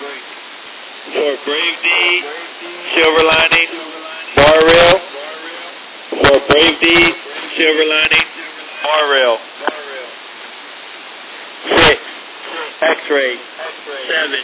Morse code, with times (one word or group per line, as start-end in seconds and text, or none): none